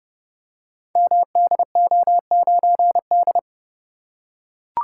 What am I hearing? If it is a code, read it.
Morse code, heard as a keyed tone, about 30 wpm, 700 Hz. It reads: MDO9D
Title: Code Groups - 5 Letter:Numbers - 30wpm